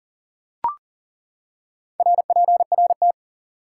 Morse code, heard as a keyed tone, about 40 wpm, 700 Hz. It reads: RPRT
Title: Top 90 QSO Elements 40wpm